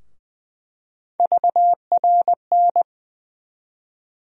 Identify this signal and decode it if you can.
Morse code, heard as a keyed tone, about 20 wpm, 700 Hz. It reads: VRN
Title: USA Call Sign Postfixes - 3 Letters 20wpm